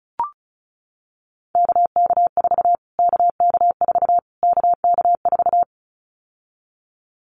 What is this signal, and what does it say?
Morse code, heard as a keyed tone, about 34 wpm, 700 Hz. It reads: KK4 KK4 KK4
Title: Callsign Prefixes for Top 25 DXCC - Repeated 3x - 35wpm